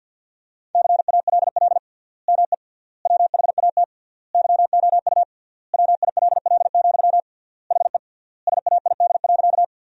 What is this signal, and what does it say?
Morse code, heard as a keyed tone, about 52 wpm, 700 Hz. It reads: CALL ME WHAT YOU WILL, HE SAID.
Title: 3x Word-Spacing - Sentences from Top 200 Words - 50wpm